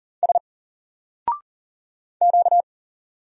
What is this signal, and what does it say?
Morse code, heard as a keyed tone, about 40 wpm, 700 Hz. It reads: S Q
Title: Single Letter-Number 40wpm